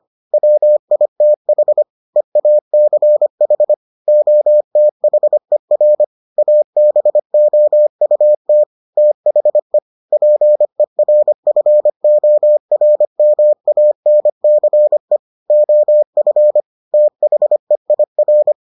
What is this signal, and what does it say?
Morse code, heard as a keyed tone, about 25 wpm, 600 Hz. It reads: WITH EACH OTHER ABOUT THE PERFORMANCE OF THEIR